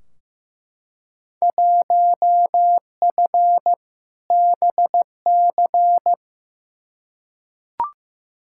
Morse code, heard as a keyed tone, about 15 wpm, 700 Hz. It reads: 1F BC
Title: Field Day Exchanges Encore - 15wpm